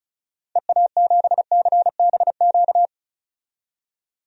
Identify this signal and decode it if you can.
Morse code, heard as a keyed tone, about 36 wpm, 700 Hz. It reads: EA7CBQ
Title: DXCC-100 Call Signs - 6 Characters - 35wpm